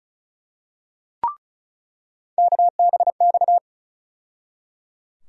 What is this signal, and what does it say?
Morse code, heard as a keyed tone, about 34 wpm, 700 Hz. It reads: KBX
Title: USA Call Sign Postfixes - 3 Letters 35wpm